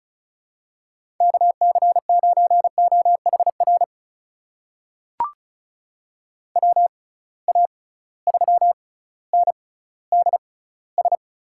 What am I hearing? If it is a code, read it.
Morse code, heard as a keyed tone, about 34 wpm, spaced as for 18 wpm, 700 Hz. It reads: KC9OHR WA3NDS
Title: USA Call Signs - 6 Characters - 6x Character Spacing - 35wpm